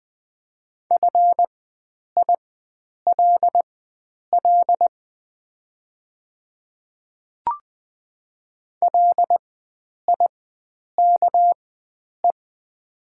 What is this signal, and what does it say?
Morse code, heard as a keyed tone, about 20 wpm, spaced as for 9 wpm, 690 Hz. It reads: FILL LIKE
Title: 4x Character Spacing - Top 500 Words - 20wpm